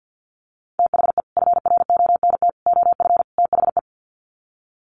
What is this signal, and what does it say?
Morse code, heard as a keyed tone, about 50 wpm, 700 Hz. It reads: THE FRONT OF THE
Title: Sets of 4 Words 50wpm